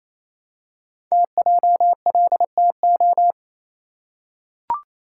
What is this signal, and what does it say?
Morse code, heard as a keyed tone, about 29 wpm, 700 Hz. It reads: TJLTO
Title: Code Groups - 5 Letters - 28wpm